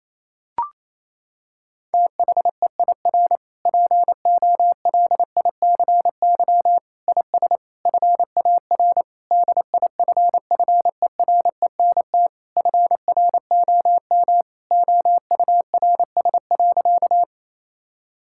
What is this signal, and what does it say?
Morse code, heard as a keyed tone, about 28 wpm, 700 Hz. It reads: THEIR POLICY IS FAR DIFFERENT FROM OURS.
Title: Sentences from Top 1000 Words - Review Part 5 - 28wpm